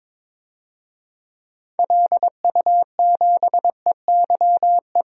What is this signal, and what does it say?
Morse code, heard as a keyed tone, about 22 wpm, 700 Hz. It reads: LU7EYE
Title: DXCC-100 Call Signs - 6 Characters - 22wpm